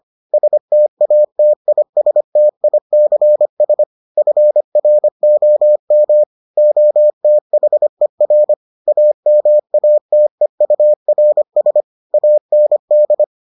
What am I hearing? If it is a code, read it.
Morse code, heard as a keyed tone, about 25 wpm, 600 Hz. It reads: STATISTICS FROM OTHER AMATEURS AND